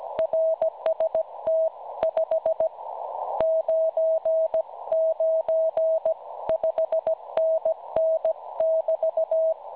Morse code, heard as a keyed tone, about 16 wpm, spaced as for 14 wpm, 650 Hz. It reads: RST 599 5NN <BT>
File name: cwronde / websdr recording .6kHz